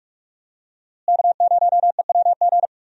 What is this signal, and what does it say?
Morse code, heard as a keyed tone, about 46 wpm, 700 Hz. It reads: K0EWG
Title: USA Call Signs - 5 Characters - Repeated 3x - 45wpm